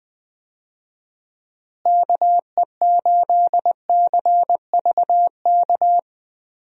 Morse code, heard as a keyed tone, about 20 wpm, 700 Hz. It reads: KE8CVK
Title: USA Call Signs - 6 Characters - Repeated 2x - 20wpm